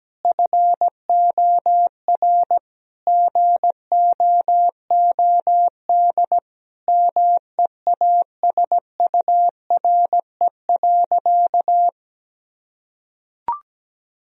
Morse code, heard as a keyed tone, about 17 wpm, 700 Hz. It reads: FOR GOOD MEASURE.
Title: Sentences from Top 1000 Words - Review Part 2 - 17wpm